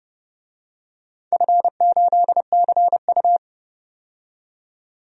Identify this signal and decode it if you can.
Morse code, heard as a keyed tone, about 32 wpm, 700 Hz. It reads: F8CU
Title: DXCC-100 Call Signs - 4 Characters - 30wpm